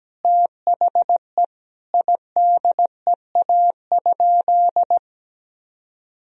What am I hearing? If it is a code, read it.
Morse code, heard as a keyed tone, about 17 wpm, 700 Hz. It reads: THE IDEA?
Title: Sentences from Top 1000 Words - Review Part 1 - 17wpm